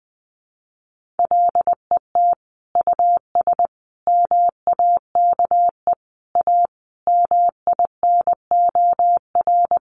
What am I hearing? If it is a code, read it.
Morse code, heard as a keyed tone, about 20 wpm, 700 Hz. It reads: LET US MAKE A MINOR
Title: Sentences from Top 1500 Words 20wpm